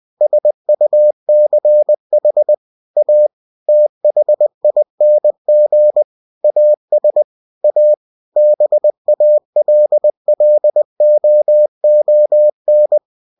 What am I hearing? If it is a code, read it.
Morse code, heard as a keyed tone, about 20 wpm, 600 Hz. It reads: SUCH A THING AS A BALLOON
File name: cw-books / winnie-the-pooh / winnie-20wpm-page-0005